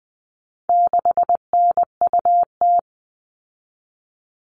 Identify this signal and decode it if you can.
Morse code, heard as a keyed tone, about 20 wpm, 700 Hz. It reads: 6NUT
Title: All Cut Numbers - 4-Digit 20wpm